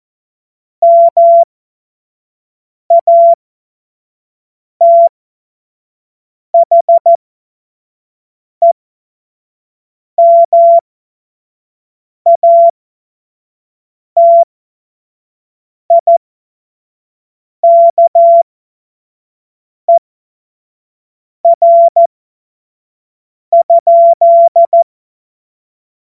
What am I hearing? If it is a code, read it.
Morse code, heard as a keyed tone, about 14 wpm, spaced as for 5 wpm, 680 Hz. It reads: MATHEMATIKER?
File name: Morsekurs / Witze / Witz 14. Dezember 2021, 25 F70